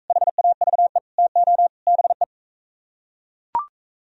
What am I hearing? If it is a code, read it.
Morse code, heard as a keyed tone, about 41 wpm, 700 Hz. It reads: HAVE TO BE